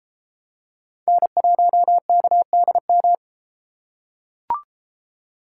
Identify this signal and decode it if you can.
Morse code, heard as a keyed tone, about 33 wpm, 700 Hz. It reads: N1KDM